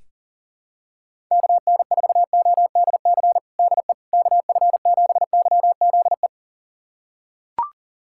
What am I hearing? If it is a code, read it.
Morse code, heard as a keyed tone, about 40 wpm, 700 Hz. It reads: KN4ODC DE KF7YZE
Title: USA Callsign DE Callsign 40wpm